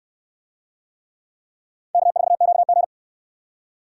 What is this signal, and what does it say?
Morse code, heard as a keyed tone, about 69 wpm, 700 Hz. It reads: K4QG